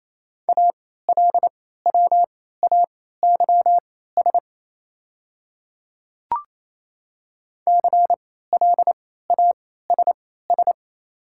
Morse code, heard as a keyed tone, about 28 wpm, spaced as for 16 wpm, 700 Hz. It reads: ALWAYS CLASS
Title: 3x Character Spacing - Top 500 Words - 28wpm